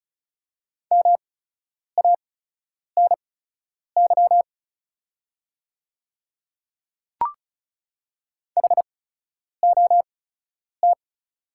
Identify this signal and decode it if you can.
Morse code, heard as a keyed tone, about 34 wpm, spaced as for 10 wpm, 690 Hz. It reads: MANY HOT
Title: 8x Character Spacing - Top 100 Words - 35wpm